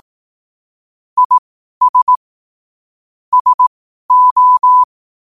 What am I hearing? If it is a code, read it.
Morse code, heard as a keyed tone, about 18 wpm, spaced as for 12 wpm, 1000 Hz. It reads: IS SO
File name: qotdfiles / 12wpm